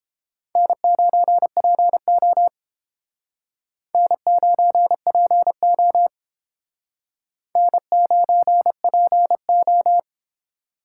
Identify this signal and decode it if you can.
Morse code, heard as a keyed tone, about 31 wpm, 700 Hz. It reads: N9PO N9PO N9PO